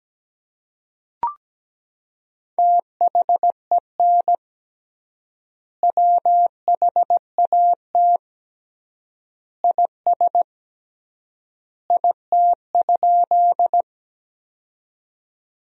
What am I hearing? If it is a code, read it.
Morse code, heard as a keyed tone, about 17 wpm, 700 Hz. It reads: THEN WHAT IS IT?
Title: 3x Word-Spacing - Sentences from Top 100 Words - 17wpm